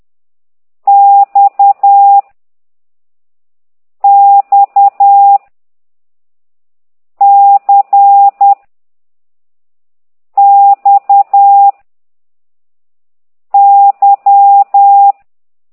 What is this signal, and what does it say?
Morse code, heard as a keyed tone, about 10 wpm, 800 Hz. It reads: X X C X Y